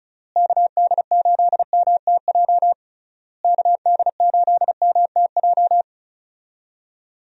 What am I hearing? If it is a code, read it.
Morse code, heard as a keyed tone, about 36 wpm, 700 Hz. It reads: KD8MTJ KD8MTJ
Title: USA Call Signs - 6 Characters - Repeated 3x - 35wpm